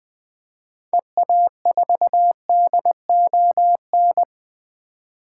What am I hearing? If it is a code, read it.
Morse code, heard as a keyed tone, about 20 wpm, 700 Hz. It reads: EA4DON
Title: 2023 SOTA Activator Call Signs - 20wpm